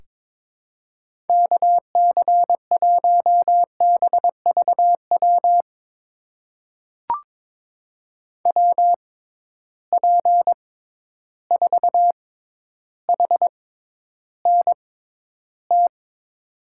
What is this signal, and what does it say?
Morse code, heard as a keyed tone, about 22 wpm, spaced as for 11 wpm, 700 Hz. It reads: KC1BVW WP4HNT
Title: USA Call Signs - 6 Characters - 6x Character Spacing - 22wpm